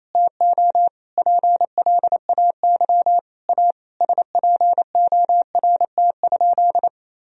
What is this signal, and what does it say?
Morse code, heard as a keyed tone, about 28 wpm, 700 Hz. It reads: TO PLAY A SPORT?
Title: Sentences from Top 1000 Words - Review Part 4 - 28wpm